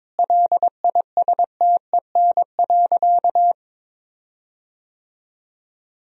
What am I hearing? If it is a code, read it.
Morse code, heard as a keyed tone, about 22 wpm, 700 Hz. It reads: LISTEN.
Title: 4x Word-Spacing - Sentences from Top 400 Words - 22wpm